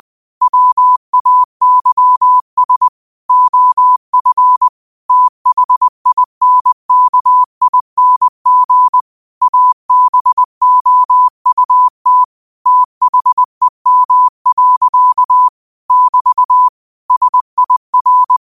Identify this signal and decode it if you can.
Morse code, heard as a keyed tone, about 20 wpm, 1000 Hz. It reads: WAYS OF THINKING ABOUT THEM. <BT> SIR
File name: qotdfiles / 20wpm